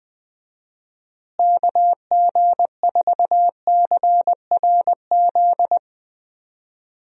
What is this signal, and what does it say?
Morse code, heard as a keyed tone, about 20 wpm, 700 Hz. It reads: KG4CRZ
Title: USA Call Signs - 6 Characters - Repeated 2x - 20wpm